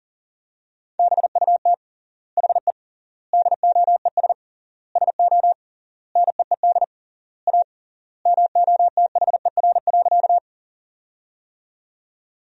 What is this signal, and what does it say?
Morse code, heard as a keyed tone, about 40 wpm, 700 Hz. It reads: BUT HE DOES SO NEED A MOTHER.
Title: 3x Word-Spacing - Sentences from Top 200 Words - 40wpm